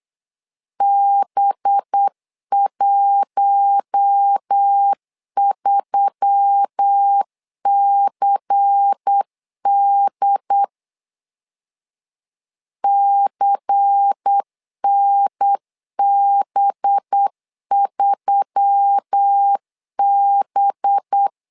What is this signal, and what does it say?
Morse code, heard as a keyed tone, about 8 wpm, 800 Hz. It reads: B13CD CNB3B
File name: LEZIONI / 5 LBVC / 5 LBVC 8wpm 2935